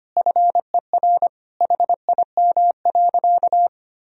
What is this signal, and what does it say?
Morse code, heard as a keyed tone, about 25 wpm, 700 Hz. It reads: FER HIM.
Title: Sentences from Top 500 Words and Abbreviations 25wpm